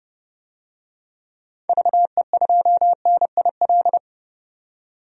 Morse code, heard as a keyed tone, about 30 wpm, 700 Hz. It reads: VE2NIL
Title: DXCC-100 Call Signs - 6 Characters - 30wpm